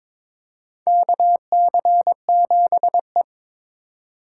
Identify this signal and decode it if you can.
Morse code, heard as a keyed tone, about 22 wpm, 700 Hz. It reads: KC7E
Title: USA Call Signs - 4 Characters - 4x Character Spacing - 22wpm